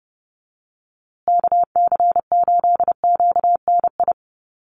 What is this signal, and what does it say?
Morse code, heard as a keyed tone, about 30 wpm, 700 Hz. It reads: KC8QNI
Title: USA Call Signs Repeated 2x 30wpm